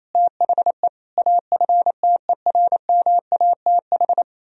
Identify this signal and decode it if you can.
Morse code, heard as a keyed tone, about 38 wpm, 700 Hz. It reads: THE AFTERMATH